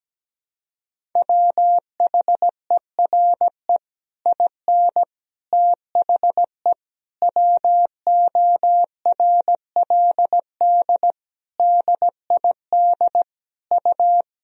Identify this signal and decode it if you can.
Morse code, heard as a keyed tone, about 17 wpm, 700 Hz. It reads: WHERE IN THE WORLD DID U
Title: Sentences from Top 200 Words and Abbreviations 17wpm